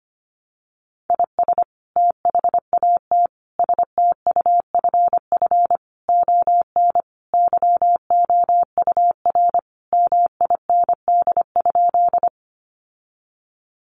Morse code, heard as a keyed tone, about 25 wpm, 700 Hz. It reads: IS THAT STUFF ON YOUR MIND?
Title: Sentences from Top 1400 Words 25wpm